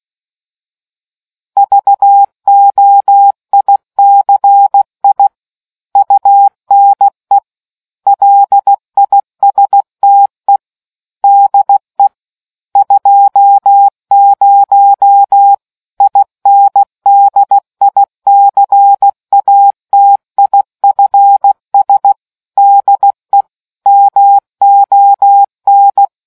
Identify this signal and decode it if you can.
Morse code, heard as a keyed tone, about 16 wpm, 800 Hz. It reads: VOICI UNE LISTE DE 20 INDICATIFS DE MON